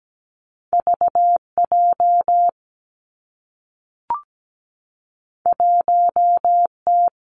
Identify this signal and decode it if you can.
Morse code, heard as a keyed tone, about 17 wpm, 700 Hz. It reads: VJ 1T
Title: Two Challenging Letter-Number Combinations 17wpm